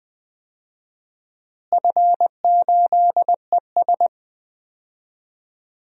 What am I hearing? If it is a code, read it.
Morse code, heard as a keyed tone, about 21 wpm, 700 Hz. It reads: F8ES
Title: DXCC-100 Call Signs - 4 Characters - Repeated 2x - 20wpm